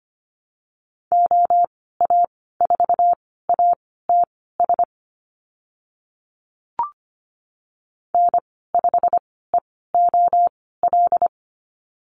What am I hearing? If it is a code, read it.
Morse code, heard as a keyed tone, about 25 wpm, spaced as for 16 wpm, 700 Hz. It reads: OA4ATS N5EOL
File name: Lesson 15g - Number 5 - Call Signs 25wpm